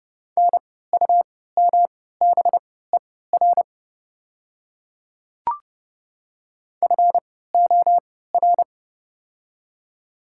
Morse code, heard as a keyed tone, about 30 wpm, spaced as for 17 wpm, 700 Hz. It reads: NUMBER FOR
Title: 3x Character Spacing - Top 100 Words - 30wpm